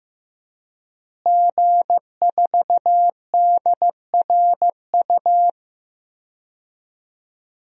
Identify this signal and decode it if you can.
Morse code, heard as a keyed tone, about 15 wpm, 700 Hz. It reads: G4DRU
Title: DXCC-100 Call Signs - 5 Characters - Repeated 3x - 15wpm